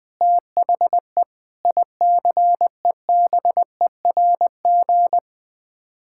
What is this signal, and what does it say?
Morse code, heard as a keyed tone, about 20 wpm, 700 Hz. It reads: THE ICEBERG
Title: Sets of 8 Words - 20wpm